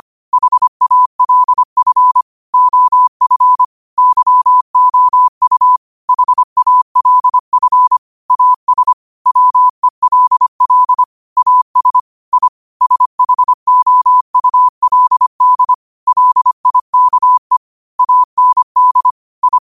Morse code, heard as a keyed tone, about 25 wpm, 1000 Hz. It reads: HALF OF YOU HALF AS WELL AS I SHOULD LIKE AND I
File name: qotdfiles / 25wpm